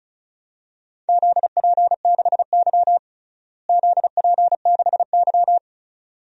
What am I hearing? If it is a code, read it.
Morse code, heard as a keyed tone, about 36 wpm, 700 Hz. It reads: ZP6Y ZP6Y
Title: DXCC-100 Call Signs - 4 Characters - Repeated 2x - 35wpm